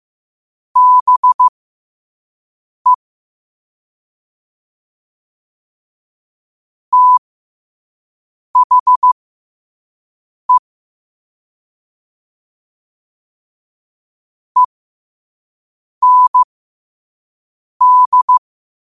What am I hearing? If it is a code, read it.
Morse code, heard as a keyed tone, about 15 wpm, spaced as for 5 wpm, 1000 Hz. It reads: BE THE END
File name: qotdfiles / 5wpm